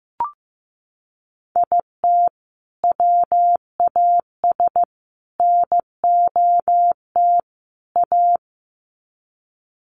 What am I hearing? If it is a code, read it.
Morse code, heard as a keyed tone, about 15 wpm, 700 Hz. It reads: IT WAS NOT A
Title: Sets of 4 Words 15wpm